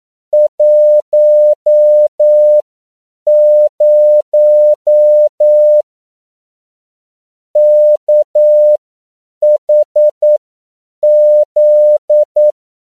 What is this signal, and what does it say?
Morse code, heard as a keyed tone, about 9 wpm, spaced as for 7 wpm, 600 Hz. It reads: 10 KHZ
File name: CW / 10PPM